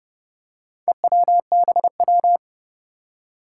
Morse code, heard as a keyed tone, about 30 wpm, 700 Hz. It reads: EWBW
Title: Four Challenging Letter-Number Combinations 30wpm